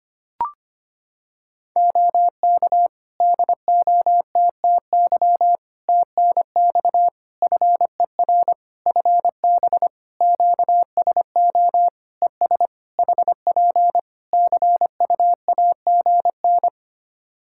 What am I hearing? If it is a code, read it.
Morse code, heard as a keyed tone, about 25 wpm, 700 Hz. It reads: OK DOTTY TNX FER FB QSO ES HP CUAGN
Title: Practice QSO - Exchange 4 - Ending - 25wpm